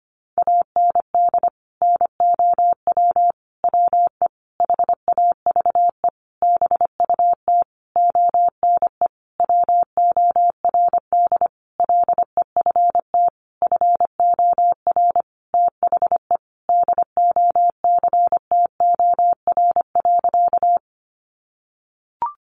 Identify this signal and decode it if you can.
Morse code, heard as a keyed tone, about 25 wpm, 700 Hz. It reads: AND NOW WE HAVE BUT ONE WORD LEFT FOR THE DOCTOR.
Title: Sentences from Top 1500 Words 25wpm